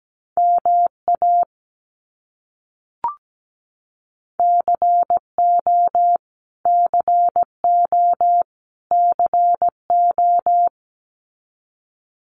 Morse code, heard as a keyed tone, about 17 wpm, 700 Hz. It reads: MA CO CO CO
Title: US States Repeated 3x 17wpm